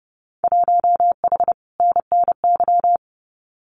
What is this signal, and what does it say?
Morse code, heard as a keyed tone, about 30 wpm, 700 Hz. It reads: 1H NNY
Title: Winter Field Day Exchanges 30wpm